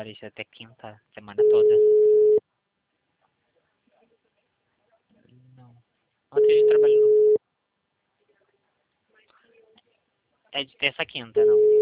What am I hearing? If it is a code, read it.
Ringback tone (Europe, 425 Hz), with speech.